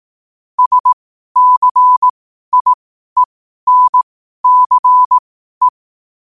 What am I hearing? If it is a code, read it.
Morse code, heard as a keyed tone, about 18 wpm, spaced as for 12 wpm, 1000 Hz. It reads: SCIENCE
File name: qotdfiles / 12wpm